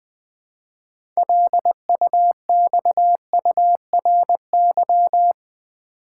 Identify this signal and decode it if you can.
Morse code, heard as a keyed tone, about 20 wpm, 700 Hz. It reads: LUXURY
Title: 6 Letter Words - First 3 Letters Given - 20wpm